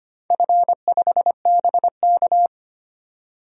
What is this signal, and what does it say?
Morse code, heard as a keyed tone, about 25 wpm, 700 Hz. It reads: F5BK
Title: DXCC-100 Call Signs - 4 Characters - 25wpm